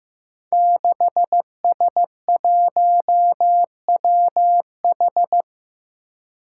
Morse code, heard as a keyed tone, about 15 wpm, 700 Hz. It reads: 6S1WH
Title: Five Challenging Letter-Number Combinations 15wpm